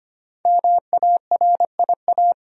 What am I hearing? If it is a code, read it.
Morse code, heard as a keyed tone, about 25 wpm, 700 Hz. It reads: MARIA
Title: USA Common Names - Max 5 Letters - 25wpm